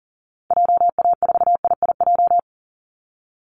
Morse code, heard as a keyed tone, about 40 wpm, 700 Hz. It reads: JA4IIJ
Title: Call Signs Repeated 2x - 40wpm